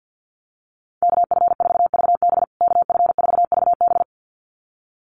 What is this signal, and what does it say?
Morse code, heard as a keyed tone, about 50 wpm, 700 Hz. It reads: KF4VB KF4VB
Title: USA Call Signs Repeated 2x 50wpm